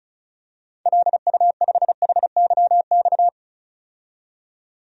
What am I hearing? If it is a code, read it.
Morse code, heard as a keyed tone, about 36 wpm, 700 Hz. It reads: LU5HYX
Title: DXCC-100 Call Signs - 6 Characters - 35wpm